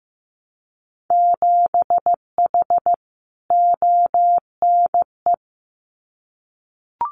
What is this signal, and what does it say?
Morse code, heard as a keyed tone, about 15 wpm, 700 Hz. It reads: 7H ONE
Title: Winter Field Day Exchanges Encore 15wpm